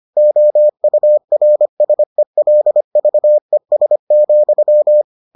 Morse code, heard as a keyed tone, about 25 wpm, 600 Hz. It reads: OURSELVES,